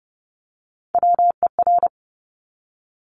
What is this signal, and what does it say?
Morse code, heard as a keyed tone, about 30 wpm, 700 Hz. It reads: WER
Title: Top Three Letter Combinations in Words 30wpm